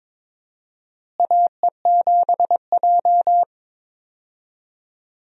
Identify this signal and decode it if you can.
Morse code, heard as a keyed tone, about 22 wpm, 700 Hz. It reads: AE7J
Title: USA Call Signs - 4 Characters - Repeated 2x - 22wpm